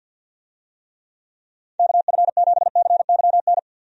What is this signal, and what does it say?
Morse code, heard as a keyed tone, about 49 wpm, 700 Hz. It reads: KF7CYN
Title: USA Call Signs - 6 Characters - 50wpm